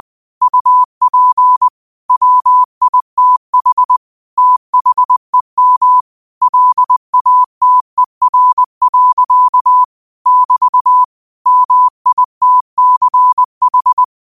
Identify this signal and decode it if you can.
Morse code, heard as a keyed tone, about 20 wpm, 1000 Hz. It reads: UP WITH THEM LATER. <BT> MITCH